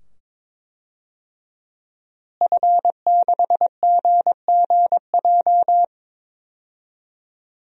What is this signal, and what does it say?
Morse code, heard as a keyed tone, about 21 wpm, 700 Hz. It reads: F6GGJ
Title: DXCC-100 Call Signs - 5 Characters - Repeated 2x - 22wpm